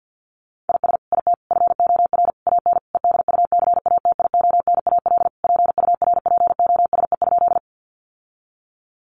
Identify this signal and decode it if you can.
Morse code, heard as a keyed tone, about 50 wpm, 700 Hz. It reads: IS IT FOR AN EDUCATIONAL PURPOSE?
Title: Sentences from Top 1200 Words 50wpm